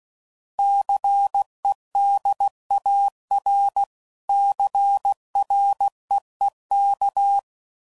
Morse code, heard as a keyed tone, about 16 wpm, 800 Hz. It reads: CEDAR CREEK
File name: CW Code Groups @15WPM / fromSM's / Cities